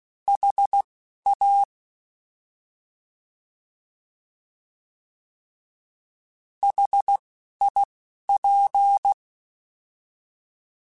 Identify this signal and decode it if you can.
Morse code, heard as a keyed tone, about 16 wpm, spaced as for 11 wpm, 800 Hz. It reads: HA HIP